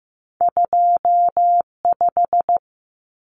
Morse code, heard as a keyed tone, about 15 wpm, 700 Hz. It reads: 25
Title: Numbers - 2-Digit 15wpm